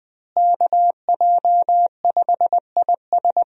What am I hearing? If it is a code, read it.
Morse code, heard as a keyed tone, about 20 wpm, 700 Hz. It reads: KJ5IS
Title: USA Call Signs - 5 Characters - Repeated 2x - 20wpm